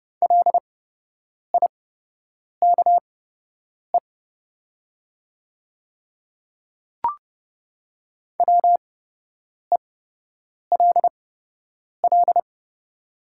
Morse code, heard as a keyed tone, about 30 wpm, spaced as for 8 wpm, 700 Hz. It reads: LIKE WELL
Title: 8x Character Spacing - Top 300 Words - 30wpm